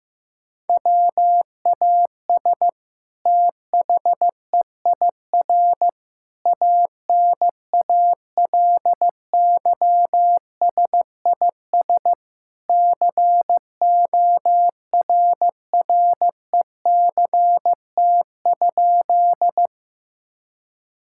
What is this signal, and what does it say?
Morse code, heard as a keyed tone, about 15 wpm, 700 Hz. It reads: WAS THEIR ANALYSIS CORRECT?
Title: Sentences from Top 1000 Words - Review Part 5 - 15wpm